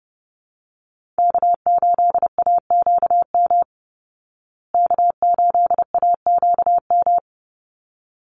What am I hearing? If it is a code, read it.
Morse code, heard as a keyed tone, about 30 wpm, 700 Hz. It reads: K8AQM K8AQM
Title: Call Signs Repeated 3x - 30wpm